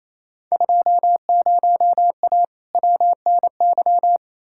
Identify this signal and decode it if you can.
Morse code, heard as a keyed tone, about 28 wpm, 700 Hz. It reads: 20A WNY
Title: Field Day Exchanges Encore - 28wpm